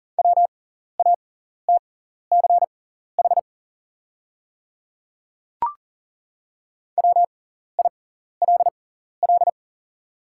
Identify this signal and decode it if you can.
Morse code, heard as a keyed tone, about 40 wpm, spaced as for 14 wpm, 700 Hz. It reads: WATCH WILL
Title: 6x Character Spacing - Top 300 Words - 40wpm